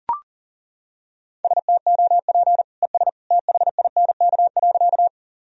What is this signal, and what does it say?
Morse code, heard as a keyed tone, about 40 wpm, 700 Hz. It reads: STOP ES THINK.
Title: Sentences from Top 300 Words and Abbreviations 40wpm